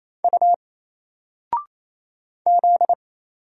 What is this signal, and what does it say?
Morse code, heard as a keyed tone, about 27 wpm, 700 Hz. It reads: U Z
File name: Single-Letter-Number-28wpm